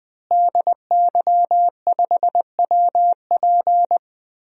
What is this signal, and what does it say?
Morse code, heard as a keyed tone, about 20 wpm, 700 Hz. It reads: DY5WP
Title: DXCC-100 Call Signs - 5 Characters - Repeated 3x - 20wpm